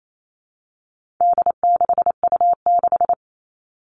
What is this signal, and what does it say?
Morse code, heard as a keyed tone, about 28 wpm, 700 Hz. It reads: D6U6
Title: All Cut Numbers - 4-Digit 28wpm